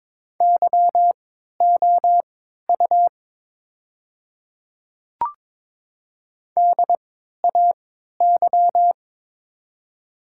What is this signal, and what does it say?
Morse code, heard as a keyed tone, about 22 wpm, spaced as for 12 wpm, 700 Hz. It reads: YOU DAY
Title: 3x Character Spacing - Top 100 Words - 22wpm